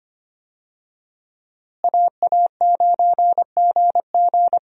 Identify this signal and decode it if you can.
Morse code, heard as a keyed tone, about 25 wpm, 700 Hz. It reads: AA9GG
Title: USA Call Signs - 5 Characters - Repeated 2x - 25wpm